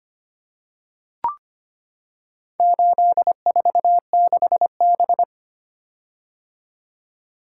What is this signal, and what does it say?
Morse code, heard as a keyed tone, about 25 wpm, 700 Hz. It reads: 846B
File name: practice / 25wpm / Four Letter-Number Combinations - 25wpm